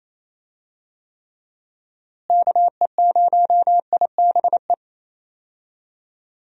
Morse code, heard as a keyed tone, about 28 wpm, 700 Hz. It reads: KE0IBE
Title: USA Call Signs - 6 Characters - Repeated 3x - 28wpm